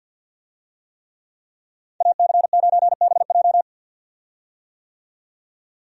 Morse code, heard as a keyed tone, about 50 wpm, 700 Hz. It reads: AC9BJ
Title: USA Call Signs - 5 Characters - 50wpm